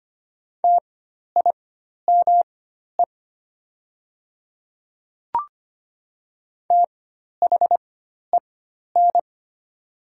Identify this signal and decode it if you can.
Morse code, heard as a keyed tone, about 25 wpm, spaced as for 12 wpm, 700 Hz. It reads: TIME THEN
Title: 4x Character Spacing - Top 100 Words - 25wpm